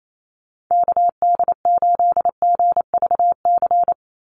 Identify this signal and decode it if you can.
Morse code, heard as a keyed tone, about 28 wpm, 700 Hz. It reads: KD8GVC
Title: USA Call Signs 28wpm